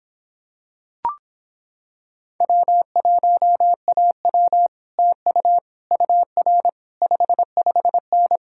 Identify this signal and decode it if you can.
Morse code, heard as a keyed tone, about 26 wpm, 700 Hz. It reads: W1AW TU UR 55N